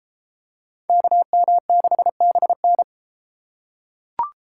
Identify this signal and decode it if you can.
Morse code, heard as a keyed tone, about 30 wpm, 700 Hz. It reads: KM6BN